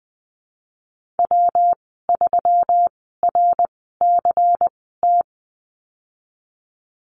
Morse code, heard as a keyed tone, about 20 wpm, spaced as for 15 wpm, 700 Hz. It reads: W3RCT
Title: 193 Lesson 31g - Letter v - Call Signs 20wpm